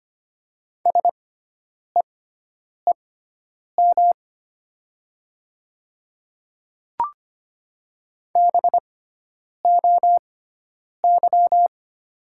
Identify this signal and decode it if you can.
Morse code, heard as a keyed tone, about 25 wpm, spaced as for 9 wpm, 700 Hz. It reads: SEEM BOY
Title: 6x Character Spacing - Top 300 Words - 25wpm